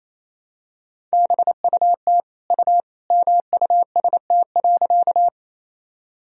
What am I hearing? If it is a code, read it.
Morse code, heard as a keyed tone, about 28 wpm, 700 Hz. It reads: BUT U MUST.
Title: Sentences from Top 200 Words and Abbreviations 28wpm